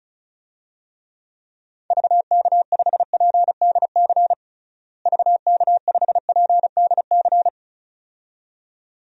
Morse code, heard as a keyed tone, about 34 wpm, 700 Hz. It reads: VK5PDC VK5PDC
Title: DXCC-100 Call Signs - 6 Characters - Repeated 2x - 35wpm